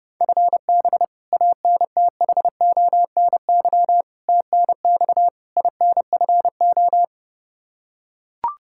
Morse code, heard as a keyed tone, about 32 wpm, 700 Hz. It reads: FB ANTHONY TNX INFO
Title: Practice QSO - Exchange 3 - More Info - 30wpm